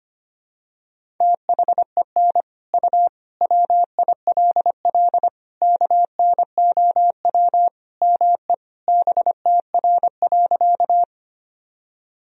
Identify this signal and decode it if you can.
Morse code, heard as a keyed tone, about 25 wpm, 700 Hz. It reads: THEN U WILL KNOW ME BTR.
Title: Sentences from Top 400 Words and Abbreviations 25wpm